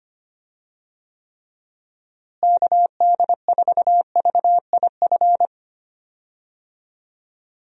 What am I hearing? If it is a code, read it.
Morse code, heard as a keyed tone, about 25 wpm, 700 Hz. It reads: KD4VIF
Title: USA Call Signs - 6 Characters - Repeated 3x - 25wpm